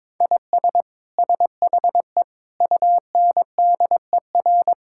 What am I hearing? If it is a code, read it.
Morse code, heard as a keyed tone, about 22 wpm, 700 Hz. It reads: IS SHE UNDER
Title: Sentences from Top 1000 Words - Review Part 6 - 22wpm